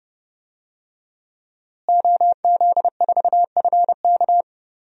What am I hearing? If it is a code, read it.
Morse code, heard as a keyed tone, about 30 wpm, 700 Hz. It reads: OZ4FK
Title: DXCC-100 Call Signs - 5 Characters - Repeated 2x - 30wpm